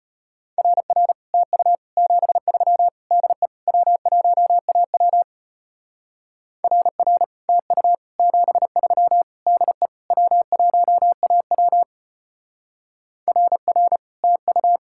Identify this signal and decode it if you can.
Morse code, heard as a keyed tone, about 32 wpm, 700 Hz. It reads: RR TU 73 DE W1AW RR TU 73 DE W1AW RR TU